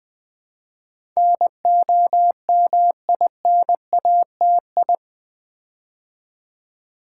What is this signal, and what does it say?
Morse code, heard as a keyed tone, about 19 wpm, 700 Hz. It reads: NOMINATI
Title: Build A Word - 4 Letters At A Time - 10-Letter Words - 20wpm